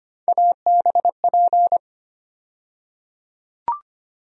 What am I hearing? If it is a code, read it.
Morse code, heard as a keyed tone, about 25 wpm, 700 Hz. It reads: ABP
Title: USA Call Sign Postfixes - 3 Letters 25wpm